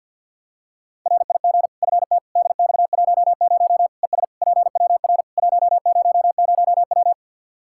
Morse code, heard as a keyed tone, about 52 wpm, 700 Hz. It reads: RIG FT DX10 ES PWR 100W
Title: Practice QSO - Exchange 2 - Info - 50wpm